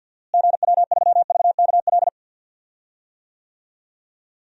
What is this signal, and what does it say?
Morse code, heard as a keyed tone, about 53 wpm, 700 Hz. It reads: GW3VKL
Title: DXCC-100 Call Signs - 6 Characters - Repeated 3x - 50wpm